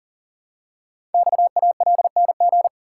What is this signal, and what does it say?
Morse code, heard as a keyed tone, about 38 wpm, 700 Hz. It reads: XALNG
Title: Code Groups - 5 Letters - 40wpm